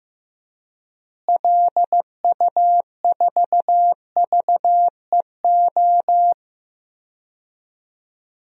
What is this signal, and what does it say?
Morse code, heard as a keyed tone, about 15 wpm, 700 Hz. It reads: LU4VEO
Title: DXCC-100 Call Signs - 6 Characters - Repeated 2x - 15wpm